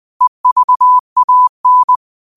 Morse code, heard as a keyed tone, about 20 wpm, 1000 Hz. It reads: EVAN
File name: qotdfiles / 20wpm